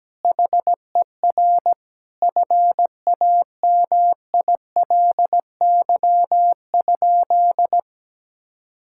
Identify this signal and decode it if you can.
Morse code, heard as a keyed tone, about 17 wpm, 700 Hz. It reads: HER FAMILY?
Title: Sentences from Top 1000 Words - Review Part 6 - 17wpm